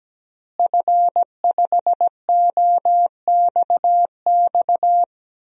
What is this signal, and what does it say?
Morse code, heard as a keyed tone, about 17 wpm, 700 Hz. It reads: F5OXX
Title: DXCC-100 Call Signs - 5 Characters - Repeated 2x - 17wpm